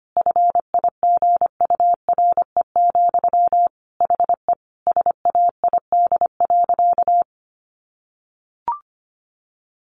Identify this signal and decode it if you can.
Morse code, heard as a keyed tone, about 25 wpm, 700 Hz. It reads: FIGURE, HE SAID.
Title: Sentences from Top 1400 Words 25wpm